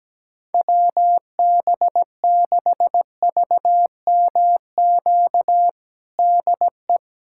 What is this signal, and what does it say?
Morse code, heard as a keyed tone, about 17 wpm, 700 Hz. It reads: WB6VMQ DE
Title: USA Callsign DE Callsign - 6 Characters - 17wpm